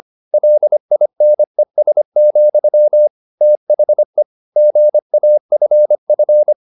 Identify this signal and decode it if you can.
Morse code, heard as a keyed tone, about 25 wpm, 600 Hz. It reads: LINES, THE GAFF
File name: the old man and the sea part 1 25wpm